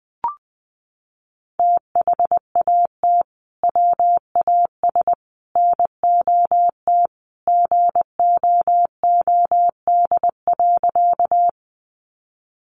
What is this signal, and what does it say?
Morse code, heard as a keyed tone, about 20 wpm, 700 Hz. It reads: THAT WAS NOT GOOD.
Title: Sentences from Top 600 Words 20wpm